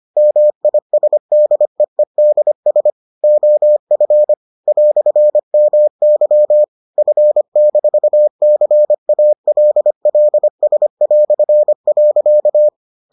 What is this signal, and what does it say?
Morse code, heard as a keyed tone, about 25 wpm, 600 Hz. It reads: MISDEEDS OF "MY F-CALLS".